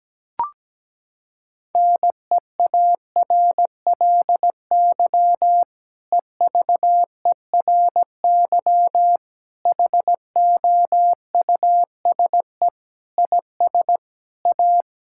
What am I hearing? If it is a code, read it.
Morse code, heard as a keyed tone, about 17 wpm, 700 Hz. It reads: NEARLY EVERY HOUSE IS A
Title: Sentences from Top 1600 Words 17wpm